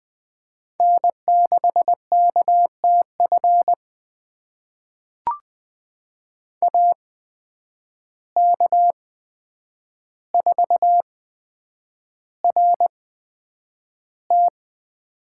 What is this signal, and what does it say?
Morse code, heard as a keyed tone, about 20 wpm, spaced as for 9 wpm, 700 Hz. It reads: N6KTF AK4RT
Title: USA Call Signs - 5 Characters - 8x Character Spacing - 20wpm